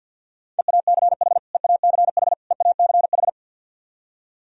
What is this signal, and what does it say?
Morse code, heard as a keyed tone, about 48 wpm, 700 Hz. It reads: EACH EACH EACH
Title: Top 100 Words Repeated 50wpm